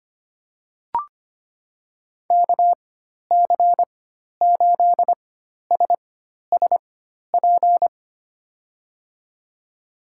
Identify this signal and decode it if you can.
Morse code, heard as a keyed tone, about 25 wpm, spaced as for 12 wpm, 700 Hz. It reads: KC8SSP
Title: USA Call Signs - 6 Characters - 4x Character Spacing - 25wpm